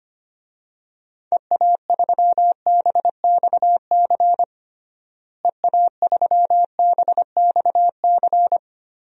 Morse code, heard as a keyed tone, about 25 wpm, 700 Hz. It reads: EA3BXC EA3BXC
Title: DXCC-100 Call Signs - 6 Characters - Repeated 3x - 25wpm